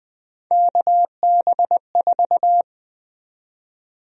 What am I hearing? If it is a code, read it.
Morse code, heard as a keyed tone, about 20 wpm, 700 Hz. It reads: KB4
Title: USA Call Sign Prefixes - 20wpm